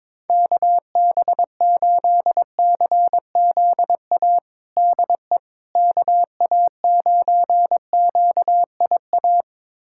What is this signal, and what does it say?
Morse code, heard as a keyed tone, about 21 wpm, 700 Hz. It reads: KB8CZA DE KA9QIA
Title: USA Callsign DE Callsign 22wpm